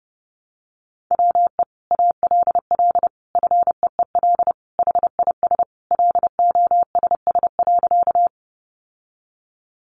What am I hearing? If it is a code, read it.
Morse code, heard as a keyed tone, about 30 wpm, 700 Hz. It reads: WE ALL FEEL HIS LOSS.
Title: Sentences from Top 1400 Words 30wpm